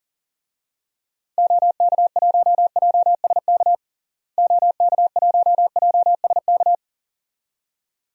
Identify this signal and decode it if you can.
Morse code, heard as a keyed tone, about 40 wpm, 700 Hz. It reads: OK1JSK OK1JSK
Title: DXCC-100 Call Signs - 6 Characters - Repeated 2x - 40wpm